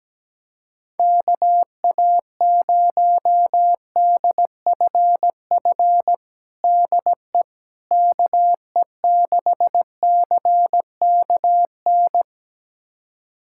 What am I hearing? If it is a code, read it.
Morse code, heard as a keyed tone, about 17 wpm, 700 Hz. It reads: KA0DFF DE KE6CKN
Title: USA Callsign DE Callsign 17wpm